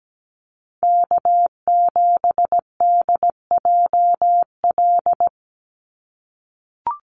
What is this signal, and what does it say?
Morse code, heard as a keyed tone, about 17 wpm, 700 Hz. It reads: K7DJL
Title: NA SOTA Chaser Callsigns 17wpm